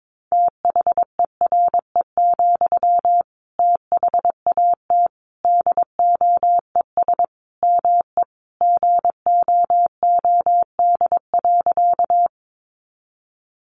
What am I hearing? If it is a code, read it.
Morse code, heard as a keyed tone, about 22 wpm, 700 Hz. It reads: THERE, THAT DOES ME GOOD.
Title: Sentences from Top 200 Words 22wpm